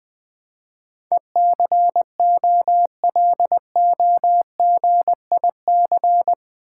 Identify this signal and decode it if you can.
Morse code, heard as a keyed tone, about 20 wpm, 700 Hz. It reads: ECOLOGIC
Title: Build A Word - 4 Letters At A Time - 10-Letter Words - 20wpm